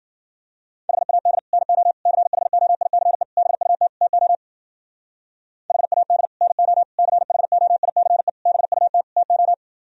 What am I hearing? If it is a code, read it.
Morse code, heard as a keyed tone, about 50 wpm, 700 Hz. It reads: HAD NO CHOICE BUT TO HAD NO CHOICE BUT TO
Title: Speed-Racing - Sets of 5 Words 40wpm